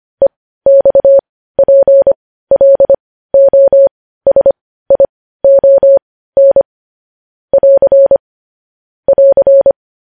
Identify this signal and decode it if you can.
Morse code, heard as a keyed tone, about 25 wpm, spaced as for 15 wpm, 550 Hz. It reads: EXPLOSION <AR> <AR>
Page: News Headlines in Morse Code at 15 WPM